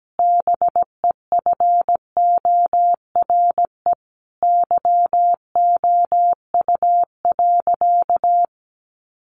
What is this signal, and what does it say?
Morse code, heard as a keyed tone, about 17 wpm, 700 Hz. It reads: BEFORE YOU.
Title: Sentences from Top 1400 Words 17wpm